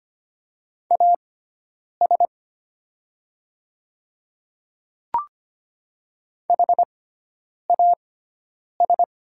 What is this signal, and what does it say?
Morse code, heard as a keyed tone, about 25 wpm, spaced as for 9 wpm, 700 Hz. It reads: AS HAS
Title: 6x Character Spacing - Top 200 Words - 25wpm